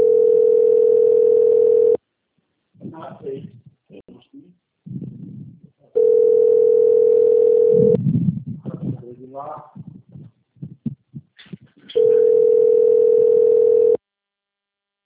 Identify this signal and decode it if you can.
Ringback tone (North America), 440 + 480 Hz, with speech.
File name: rg-1600-unknown- - - .47374